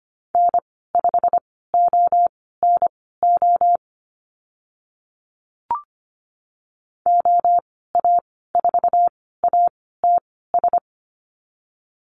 Morse code, heard as a keyed tone, about 25 wpm, spaced as for 16 wpm, 700 Hz. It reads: N5ONO OA4ATS
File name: Lesson 15g - Number 5 - Call Signs 25wpm